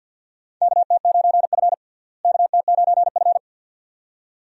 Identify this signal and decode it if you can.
Morse code, heard as a keyed tone, about 52 wpm, 700 Hz. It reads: KT9F KT9F
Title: USA Call Signs - 4 Characters - Repeated 2x - 50wpm